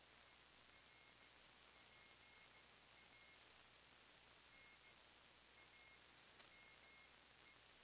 Morse code, heard as a keyed tone, about 15 wpm, spaced as for 11 wpm, 2150 Hz, with noise. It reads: ER PA NAME
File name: sdr / WINRAD WB5RVZ PSK Sample 40m